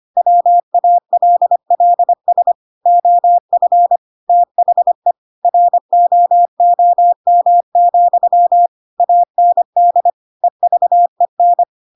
Morse code, heard as a keyed tone, about 25 wpm, 700 Hz. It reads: WALLS OF THE ROOM, AND EVEN
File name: Pinocchio / Pinocchio ch1 25 25 ik1hge